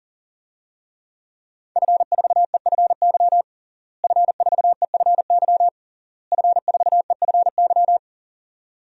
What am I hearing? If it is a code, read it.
Morse code, heard as a keyed tone, about 40 wpm, 700 Hz. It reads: F4EFY F4EFY F4EFY
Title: DXCC-100 Call Signs - 5 Characters - Repeated 3x - 40wpm